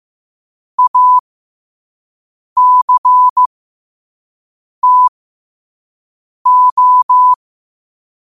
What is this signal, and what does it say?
Morse code, heard as a keyed tone, about 15 wpm, spaced as for 5 wpm, 1000 Hz. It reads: ACTO
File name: qotdfiles / 5wpm